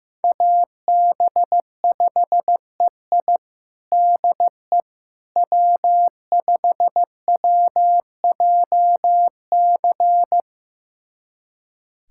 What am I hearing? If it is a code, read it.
Morse code, heard as a keyed tone, about 15 wpm, 700 Hz. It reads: AB5EI DE W5WJC
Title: USA Callsign DE Callsign - 5 Characters - 15wpm